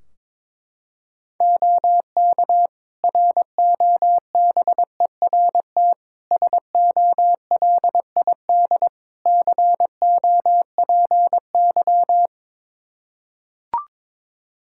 Morse code, heard as a keyed tone, about 22 wpm, 700 Hz. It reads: OK ROBERT SOLID COPY
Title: K1USN SST QSO Practice 22wpm